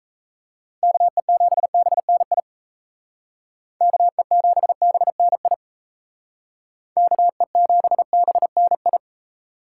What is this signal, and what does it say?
Morse code, heard as a keyed tone, about 33 wpm, 700 Hz. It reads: KE7BNI KE7BNI KE7BNI
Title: Speed Racing - USA 2x3 Call Signs 28wpm